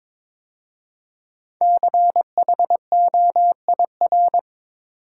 Morse code, heard as a keyed tone, about 22 wpm, 700 Hz. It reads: CHOIR
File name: 5 Letter Words - First 3 Letters Given - 22wpm